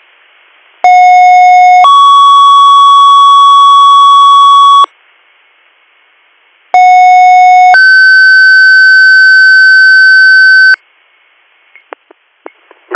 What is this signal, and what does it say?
Two-tone page, heard as a one-second tone, then a three-second tone, about 730 then 1120 Hz, about 730 then 1550 Hz. It is heard from 0.8 to 10.7 s.